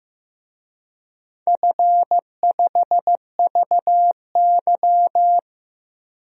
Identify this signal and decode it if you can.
Morse code, heard as a keyed tone, about 15 wpm, 700 Hz. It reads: F5VY
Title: DXCC-100 Call Signs - 4 Characters - Repeated 3x - 15wpm